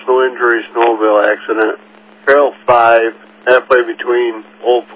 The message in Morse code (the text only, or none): none